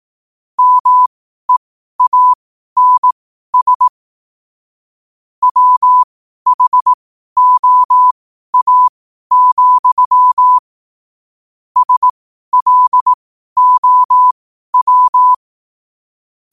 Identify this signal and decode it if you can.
Morse code, heard as a keyed tone, about 18 wpm, spaced as for 12 wpm, 1000 Hz. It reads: MEANS WHOA, SLOW
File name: qotdfiles / 12wpm